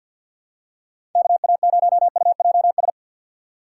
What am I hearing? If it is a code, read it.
Morse code, heard as a keyed tone, about 53 wpm, 690 Hz. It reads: KA0UJS
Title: USA Call Signs - 2x Character Spacing - 50wpm